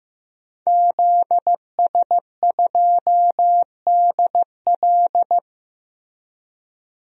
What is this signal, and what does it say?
Morse code, heard as a keyed tone, about 15 wpm, 700 Hz. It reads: ZS2DL
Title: DXCC-100 Call Signs - 5 Characters - Repeated 2x - 15wpm